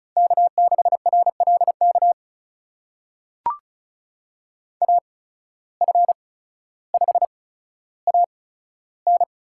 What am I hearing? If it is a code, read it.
Morse code, heard as a keyed tone, about 36 wpm, spaced as for 15 wpm, 700 Hz. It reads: K6RLK AF5AN
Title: USA Call Signs - 5 Characters - 8x Character Spacing - 35wpm